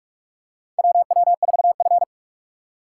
Morse code, heard as a keyed tone, about 46 wpm, 690 Hz. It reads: WW4F
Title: USA Call Signs - 4 Characters - 2x Character Spacing - 45wpm